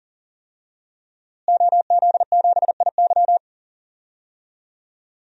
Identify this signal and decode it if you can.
Morse code, heard as a keyed tone, about 40 wpm, 700 Hz. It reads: OZ7IY
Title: DXCC-100 Call Signs - 5 Characters - Repeated 3x - 40wpm